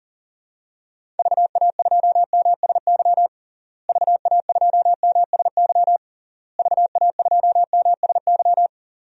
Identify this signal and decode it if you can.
Morse code, heard as a keyed tone, about 37 wpm, 700 Hz. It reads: VA2MSY VA2MSY VA2MSY
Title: DXCC-100 Call Signs - 6 Characters - Repeated 3x - 40wpm